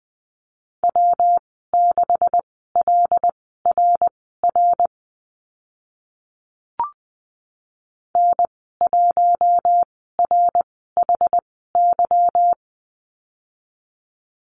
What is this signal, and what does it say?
Morse code, heard as a keyed tone, about 20 wpm, spaced as for 15 wpm, 700 Hz. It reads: W6LRR N1RHY
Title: 167 Lesson 27g - Number 7 - Call Signs 20wpm